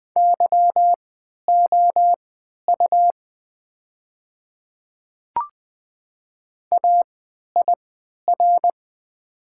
Morse code, heard as a keyed tone, about 20 wpm, spaced as for 11 wpm, 700 Hz. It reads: YOU AIR
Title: 3x Character Spacing - Top 200 Words - 20wpm